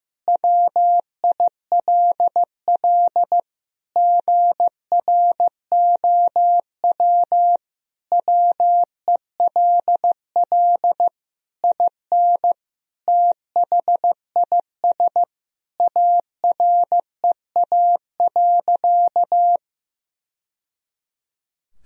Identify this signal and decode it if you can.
Morse code, heard as a keyed tone, about 15 wpm, 700 Hz. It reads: WILL GROW WELL IN THIS AREA.
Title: Sentences from Top 1000 Words - Review Part 2 - 15wpm